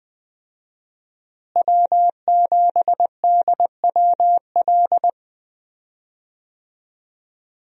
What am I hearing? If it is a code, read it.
Morse code, heard as a keyed tone, about 20 wpm, 700 Hz. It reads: W7DWL
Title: USA Call Signs - 5 Characters - Repeated 2x - 20wpm